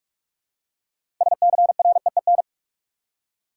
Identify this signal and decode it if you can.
Morse code, heard as a keyed tone, about 44 wpm, 700 Hz. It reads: SCREEN
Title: Six Letter Words - 45wpm